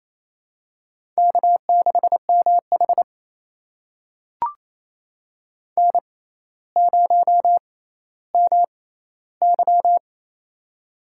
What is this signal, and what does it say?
Morse code, heard as a keyed tone, about 29 wpm, spaced as for 14 wpm, 700 Hz. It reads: K6MH N0MY
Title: USA Call Signs - 4 Characters - 6x Character Spacing - 28wpm